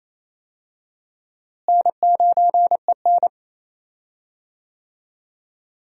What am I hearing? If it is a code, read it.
Morse code, heard as a keyed tone, about 28 wpm, 700 Hz. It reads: N9EN
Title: USA Call Signs - 4 Characters - Repeated 3x - 28wpm